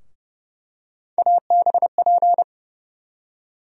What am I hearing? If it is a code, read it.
Morse code, heard as a keyed tone, about 32 wpm, 700 Hz. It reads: ABP
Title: USA Call Sign Postfixes - 3 Letters 30wpm